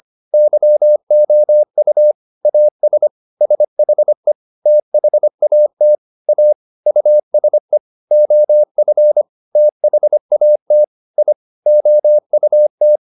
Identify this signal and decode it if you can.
Morse code, heard as a keyed tone, about 25 wpm, 600 Hz. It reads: YOU AS SHE THAT A USE OF THAT I OUT